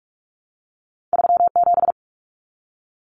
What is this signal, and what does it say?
Morse code, heard as a keyed tone, about 46 wpm, 700 Hz. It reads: 37
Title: Common Cut Numbers - 2-Digit 45wpm